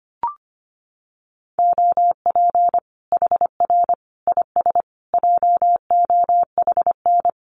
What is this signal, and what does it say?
Morse code, heard as a keyed tone, about 25 wpm, 700 Hz. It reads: OP HR IS JOHN
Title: Common QSO Phrases 25wpm